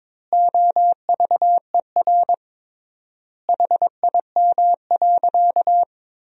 Morse code, heard as a keyed tone, about 22 wpm, 700 Hz. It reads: OVER HIM.
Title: 3x Word-Spacing - Sentences from Top 300 Words - 22wpm